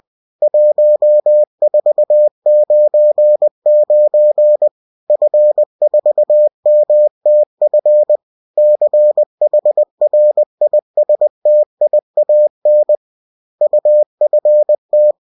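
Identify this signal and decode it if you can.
Morse code, heard as a keyed tone, about 20 wpm, 600 Hz. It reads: 1499 F4MTF CHRISTIAN UFT
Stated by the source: Bulletin sent in CW.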